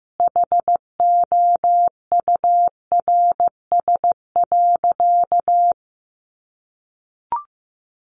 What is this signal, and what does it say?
Morse code, heard as a keyed tone, about 15 wpm, 700 Hz. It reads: HOURS.
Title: Additional Easy Phrases 15wpm